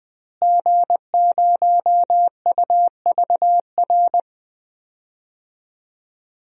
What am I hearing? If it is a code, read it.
Morse code, heard as a keyed tone, about 21 wpm, 700 Hz. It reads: G0UVR
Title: DXCC-100 Call Signs - 5 Characters - Repeated 3x - 20wpm